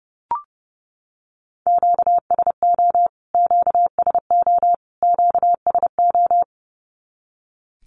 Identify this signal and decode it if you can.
Morse code, heard as a keyed tone, about 29 wpm, 700 Hz. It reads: QSO QSO QSO
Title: Q-Codes Repeated 3x 30wpm